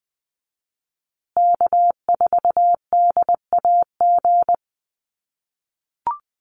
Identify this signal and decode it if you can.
Morse code, heard as a keyed tone, about 20 wpm, 700 Hz. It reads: K4DAG
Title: USA Call Signs Repeated 3x 20wpm